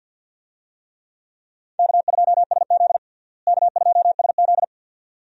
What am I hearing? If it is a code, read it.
Morse code, heard as a keyed tone, about 47 wpm, 700 Hz. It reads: K2SZ K2SZ
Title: USA Call Signs - 4 Characters - Repeated 2x - 50wpm